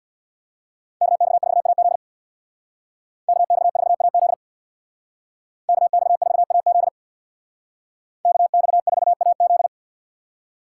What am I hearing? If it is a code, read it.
Morse code, heard as a keyed tone, about 58 wpm, 700 Hz. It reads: KX4AZ KX4AZ KX4AZ KX4AZ
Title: Speed Racing - USA 2x2 Call Signs 50wpm